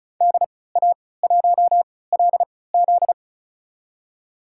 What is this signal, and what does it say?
Morse code, heard as a keyed tone, about 34 wpm, spaced as for 20 wpm, 700 Hz. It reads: DA1LZ
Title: USA Call Signs - 5 Characters - 3x Character Spacing - 35wpm